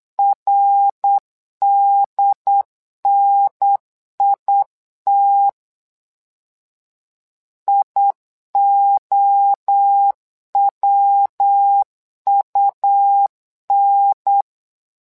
Morse code, heard as a keyed tone, about 8 wpm, 800 Hz. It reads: RDNIT IOWUN
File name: LEZIONI / 3 MWRO / 3 MWRO 8wpm 2606